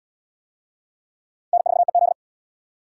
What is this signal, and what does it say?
Morse code, heard as a keyed tone, about 68 wpm, 700 Hz. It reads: N4EZ